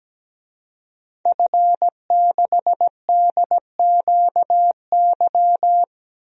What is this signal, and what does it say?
Morse code, heard as a keyed tone, about 17 wpm, 700 Hz. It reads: F6DQY
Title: DXCC-100 Call Signs - 5 Characters - Repeated 3x - 17wpm